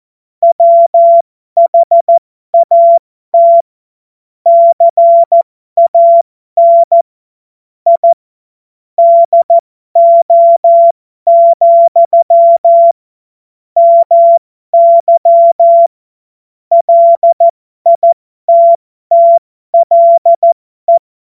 Morse code, heard as a keyed tone, about 14 wpm, spaced as for 12 wpm, 680 Hz. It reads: WHAT CAN I DO, MY LITTLE
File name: Morsekurs / Witze